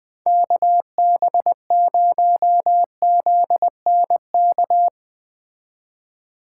Morse code, heard as a keyed tone, about 20 wpm, 700 Hz. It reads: KB0ZNK
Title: Speed Racing - USA 2x3 Call Signs 17wpm